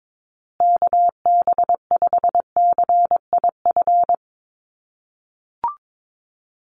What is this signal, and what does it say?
Morse code, heard as a keyed tone, about 22 wpm, 700 Hz. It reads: KB5CIF
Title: USA Call Signs Repeated 3x 22wpm